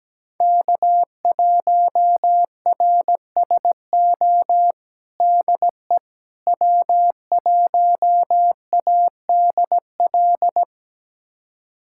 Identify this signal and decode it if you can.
Morse code, heard as a keyed tone, about 17 wpm, 700 Hz. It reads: K1RSO DE W1ADL
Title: USA Callsign DE Callsign - 5 Characters - 17wpm